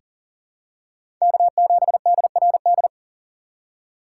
Morse code, heard as a keyed tone, about 40 wpm, 700 Hz. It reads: K7DRD
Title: USA Call Signs - 5 Characters - 3x Character Spacing - 40wpm